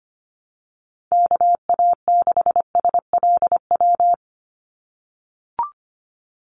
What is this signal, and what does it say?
Morse code, heard as a keyed tone, about 25 wpm, 700 Hz. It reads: KA6SLW
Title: USA Call Signs 25wpm